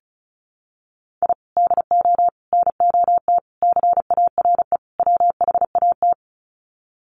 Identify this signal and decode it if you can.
Morse code, heard as a keyed tone, about 34 wpm, 700 Hz. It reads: I DO NOT CARE WHAT
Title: Sets of 5 Words 35wpm